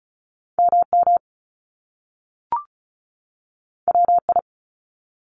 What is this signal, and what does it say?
Morse code, heard as a keyed tone, about 36 wpm, 700 Hz. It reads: MM WI
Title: Top Two Letter Combinations in Words 35wpm